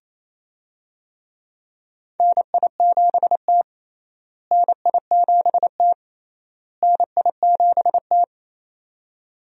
Morse code, heard as a keyed tone, about 27 wpm, 700 Hz. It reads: NI7T NI7T NI7T
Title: USA Call Signs - 4 Characters - Repeated 3x - 28wpm